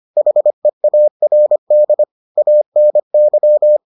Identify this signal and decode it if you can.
Morse code, heard as a keyed tone, about 25 wpm, 600 Hz. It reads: HEARD ANY